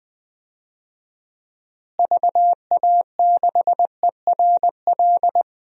Morse code, heard as a keyed tone, about 20 wpm, 700 Hz. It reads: VA6ERL
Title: DXCC-100 Call Signs - 6 Characters - Repeated 2x - 20wpm